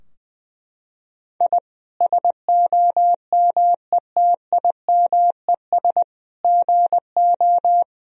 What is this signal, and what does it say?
Morse code, heard as a keyed tone, about 21 wpm, 700 Hz. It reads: I SOMETIMES GO
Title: Sentences from Top 1000 Words - Review Part 3 - 20wpm